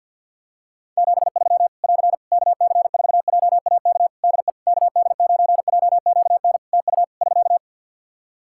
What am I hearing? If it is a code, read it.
Morse code, heard as a keyed tone, about 50 wpm, 700 Hz. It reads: 73 <AR> KC4JAK DE KD9JQN TU <SK>
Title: Practice QSO - Exchange 4 - Ending - 50wpm